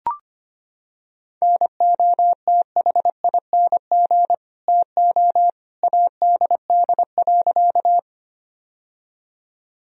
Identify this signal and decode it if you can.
Morse code, heard as a keyed tone, about 25 wpm, 700 Hz. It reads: NOTHING TO ADD.
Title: Sentences from Top 1000 Words - Review Part 1 - 25wpm